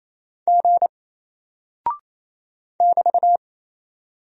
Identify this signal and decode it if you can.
Morse code, heard as a keyed tone, about 27 wpm, 700 Hz. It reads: G <BT>
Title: Single Character (Letter:Number:Punctuation:Prosign) 28wpm